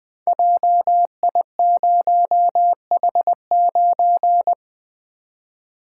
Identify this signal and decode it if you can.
Morse code, heard as a keyed tone, about 20 wpm, 700 Hz. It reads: JI0H9
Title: Code Groups - 5 Letter:Numbers - 20wpm